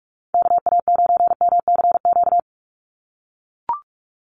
Morse code, heard as a keyed tone, about 44 wpm, 700 Hz. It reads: KA9MCQ
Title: USA Call Signs 45wpm